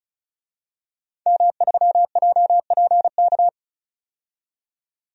Morse code, heard as a keyed tone, about 36 wpm, 700 Hz. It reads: M3JPK
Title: DXCC-100 Call Signs - 5 Characters - 35wpm